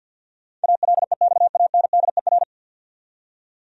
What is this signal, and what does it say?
Morse code, heard as a keyed tone, about 52 wpm, 700 Hz. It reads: ALEXANDER
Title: USA Common Names - 50wpm